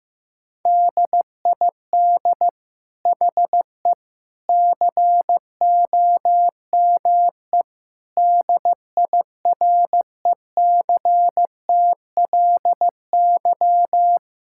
Morse code, heard as a keyed tone, about 15 wpm, 700 Hz. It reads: DID HE COME DIRECTLY
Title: Sentences from Top 1000 Words - Review Part 5 - 15wpm